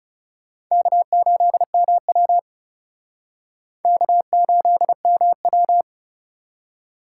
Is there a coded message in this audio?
Yes: Morse code from 0.7 to 5.8 s, heard as a keyed tone, about 38 wpm, 700 Hz. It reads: K8MW K8MW